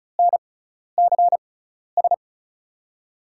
Morse code, heard as a keyed tone, about 36 wpm, spaced as for 12 wpm, 700 Hz. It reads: NCS
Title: USA Call Signs - 5 Characters - 6x Character Spacing - 35wpm